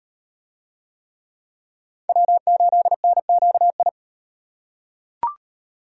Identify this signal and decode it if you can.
Morse code, heard as a keyed tone, about 38 wpm, 700 Hz. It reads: W8NQI